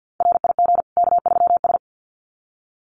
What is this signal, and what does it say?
Morse code, heard as a keyed tone, about 50 wpm, 700 Hz. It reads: RIG K3S
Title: Common QSO Phrases 50wpm